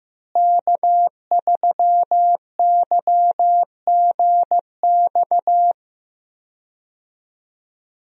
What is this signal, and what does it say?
Morse code, heard as a keyed tone, about 15 wpm, 700 Hz. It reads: K3YGX
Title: USA Call Signs - 5 Characters - Repeated 2x - 15wpm